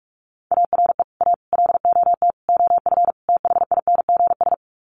Morse code, heard as a keyed tone, about 44 wpm, 700 Hz. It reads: ARE A LOT OF THINGS
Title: Sets of 5 Words Encore2 45wpm